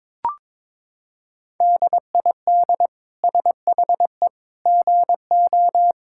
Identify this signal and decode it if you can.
Morse code, heard as a keyed tone, about 22 wpm, 700 Hz. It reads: DID SHE GO
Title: Sentences from Top 1000 Words - Review Part 6 - 22wpm